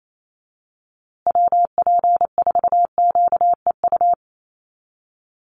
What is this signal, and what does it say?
Morse code, heard as a keyed tone, about 28 wpm, 700 Hz. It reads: WP4QEU
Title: USA Call Signs Repeated 3x 28wpm